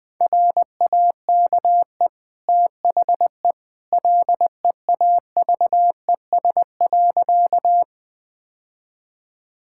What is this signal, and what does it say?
Morse code, heard as a keyed tone, about 20 wpm, 700 Hz. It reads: RAKE THE LEAVES.
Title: Easy Sentences with Hints - 20wpm